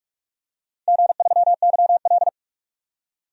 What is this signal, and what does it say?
Morse code, heard as a keyed tone, about 44 wpm, 700 Hz. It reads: G3YL
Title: DXCC-100 Call Signs - 4 Characters - 45wpm